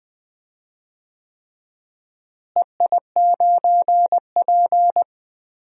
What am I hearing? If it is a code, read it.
Morse code, heard as a keyed tone, about 20 wpm, 700 Hz. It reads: EI9P
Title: DXCC-100 Call Signs - 4 Characters - Repeated 3x - 20wpm